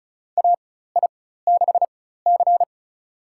Morse code, heard as a keyed tone, about 34 wpm, spaced as for 16 wpm, 700 Hz. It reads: AI6C
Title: USA Call Signs - 4x Character Spacing - 35wpm